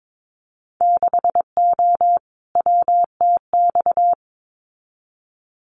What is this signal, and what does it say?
Morse code, heard as a keyed tone, about 22 wpm, 700 Hz. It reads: 6O WTX
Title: Winter Field Day Exchanges Encore 22wpm